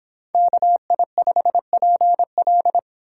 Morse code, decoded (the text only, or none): KI5PL